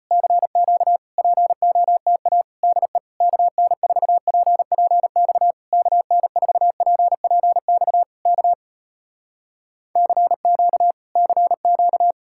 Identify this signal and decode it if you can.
Morse code, heard as a keyed tone, about 34 wpm, 700 Hz. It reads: CQ POTA DE KN4PPX KN4PPX K CQ CQ